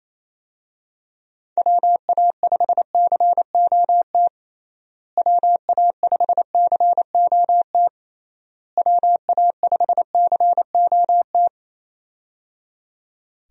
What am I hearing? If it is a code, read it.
Morse code, heard as a keyed tone, about 28 wpm, 700 Hz. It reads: WA5COT WA5COT WA5COT
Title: USA Call Signs - 6 Characters - Repeated 3x - 28wpm